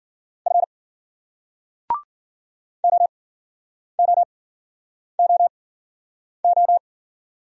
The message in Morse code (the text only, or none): F OOOO